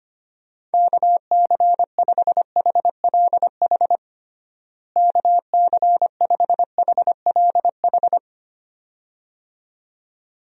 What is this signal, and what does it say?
Morse code, heard as a keyed tone, about 25 wpm, 700 Hz. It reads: KC5HLH KC5HLH
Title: USA Call Signs - 6 Characters - Repeated 2x - 25wpm